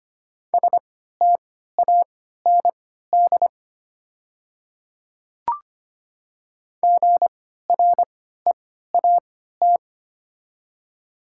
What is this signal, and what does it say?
Morse code, heard as a keyed tone, about 25 wpm, spaced as for 14 wpm, 700 Hz. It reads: STAND GREAT
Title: 3x Character Spacing - Top 300 Words - 25wpm